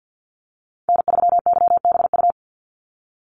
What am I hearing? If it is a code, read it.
Morse code, heard as a keyed tone, about 50 wpm, 700 Hz. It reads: N3YBU
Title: USA Call Signs 50wpm